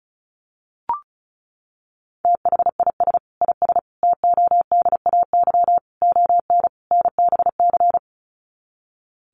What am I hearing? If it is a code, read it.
Morse code, heard as a keyed tone, about 34 wpm, 700 Hz. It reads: THIS IS TODAY ON NBC
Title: Sets of 5 Words Encore 35wpm